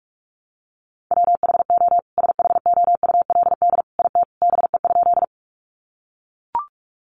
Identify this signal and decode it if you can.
Morse code, heard as a keyed tone, about 44 wpm, 700 Hz. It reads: WHO SHOULD IT BE?
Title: Sentences from Top 300 Words 45wpm